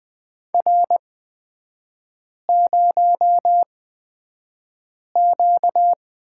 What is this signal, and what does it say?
Morse code, heard as a keyed tone, about 20 wpm, spaced as for 5 wpm, 700 Hz. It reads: R0Q
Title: 236 Lesson 38e - Letter q - Three Characters 20wpm